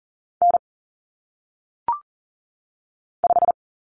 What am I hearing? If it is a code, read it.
Morse code, heard as a keyed tone, about 39 wpm, 700 Hz. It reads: N 5
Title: Common Cut Numbers - 1-Digit 40wpm